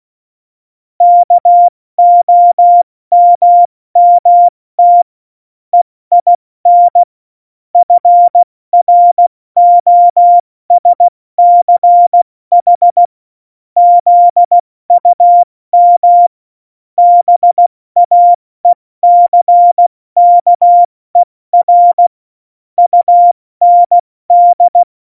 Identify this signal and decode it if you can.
Morse code, heard as a keyed tone, about 16 wpm, 700 Hz. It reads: KOMMT EIN FROSCH ZUM BAECKER UND
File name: Morsekurs / Witze / Witz 11. September 2018, 70 F80